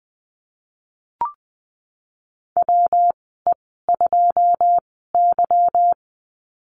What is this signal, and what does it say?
Morse code, heard as a keyed tone, about 20 wpm, spaced as for 15 wpm, 700 Hz. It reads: WE2Y
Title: 193 Lesson 31g - Letter v - Call Signs 20wpm